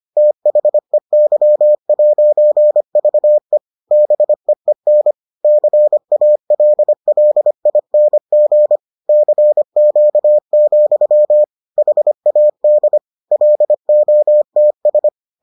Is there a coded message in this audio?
Yes: Morse code from 0.2 to 15.1 s, heard as a keyed tone, about 25 wpm, 600 Hz. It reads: THEY'VE BEEN CALLING CQ, HAD LOTS